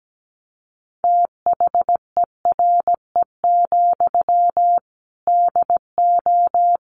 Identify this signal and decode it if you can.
Morse code, heard as a keyed tone, about 17 wpm, 700 Hz. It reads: THERE, DO
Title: Sentences from Top 300 Words 17wpm